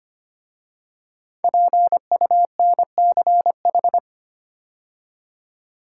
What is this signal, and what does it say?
Morse code, heard as a keyed tone, about 25 wpm, 700 Hz. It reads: PUNCH
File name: 5 Letter Words - First 3 Letters Given - 25wpm